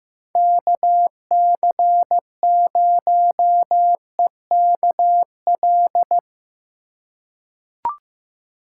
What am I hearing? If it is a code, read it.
Morse code, heard as a keyed tone, about 15 wpm, 700 Hz. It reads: KC0EKL
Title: USA Call Signs - 6 Characters - 3x Character Spacing - 15wpm